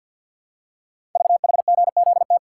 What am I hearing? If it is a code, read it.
Morse code, heard as a keyed tone, about 50 wpm, 690 Hz. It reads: VHGZT
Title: Code Groups - 5 Letters - 50wpm